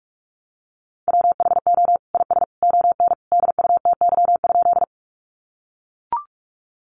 Morse code, heard as a keyed tone, about 44 wpm, 700 Hz. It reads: WHO IS ON DUTY?
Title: Sentences from Top 1500 Words 45wpm